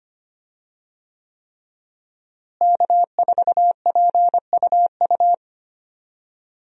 Morse code, heard as a keyed tone, about 25 wpm, 700 Hz. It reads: K4PUU
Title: USA Call Signs - 5 Characters - Repeated 3x - 25wpm